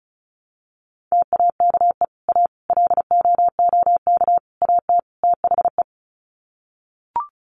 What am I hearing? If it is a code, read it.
Morse code, heard as a keyed tone, about 34 wpm, 700 Hz. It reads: TAKE A LOOK AT THE
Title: Sets of 5 Words 35wpm